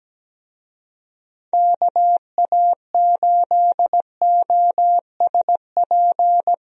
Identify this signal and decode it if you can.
Morse code, heard as a keyed tone, about 17 wpm, 700 Hz. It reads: KA8OSP
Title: USA Call Signs - 6 Characters - Repeated 2x - 17wpm